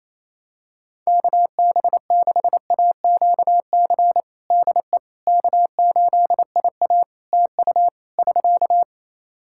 Morse code, heard as a keyed tone, about 28 wpm, 700 Hz. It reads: KB6AQC DE K8IA TU <SK>
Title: K1USN SST QSO Practice 28wpm